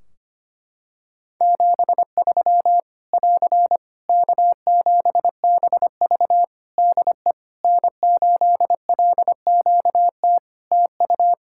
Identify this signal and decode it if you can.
Morse code, heard as a keyed tone, about 25 wpm, 700 Hz. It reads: 73 <AR> K7BV DE N8LQT TU
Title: Practice QSO - Exchange 4 - Ending - 25wpm